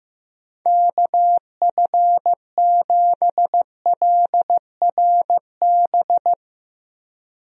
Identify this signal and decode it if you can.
Morse code, heard as a keyed tone, about 15 wpm, 700 Hz. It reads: KF7LRB
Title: USA Call Signs - 6 Characters - Repeated 2x - 15wpm